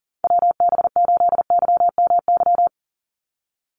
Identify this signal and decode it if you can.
Morse code, heard as a keyed tone, about 38 wpm, 700 Hz. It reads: WB8YMY
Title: USA Call Signs 40wpm